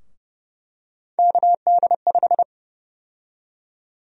Morse code, heard as a keyed tone, about 30 wpm, 700 Hz. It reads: KD5
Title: USA Call Sign Prefixes - 30wpm